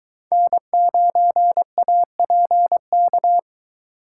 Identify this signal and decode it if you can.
Morse code, heard as a keyed tone, about 20 wpm, 700 Hz. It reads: N9APK